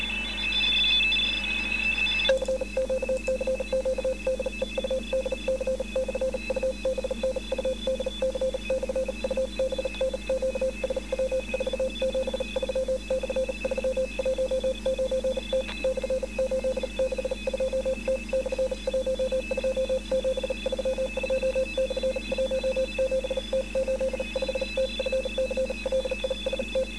Morse code, heard as a keyed tone, about 40 wpm, spaced as for 34 wpm, 520 Hz, with noise. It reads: CQ CQ DE UDC/UBN UDC/UBN QSW 473/319T /862T/12732/17T85T FCLIST